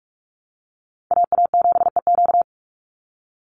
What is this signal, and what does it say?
Morse code, heard as a keyed tone, about 44 wpm, 700 Hz. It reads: AA7EQ
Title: NA SOTA Chaser Callsigns 45wpm